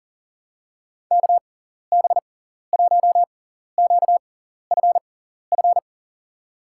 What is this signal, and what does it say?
Morse code, heard as a keyed tone, about 37 wpm, spaced as for 14 wpm, 700 Hz. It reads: KB1QFF
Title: USA Call Signs - 6x Character Spacing - 40wpm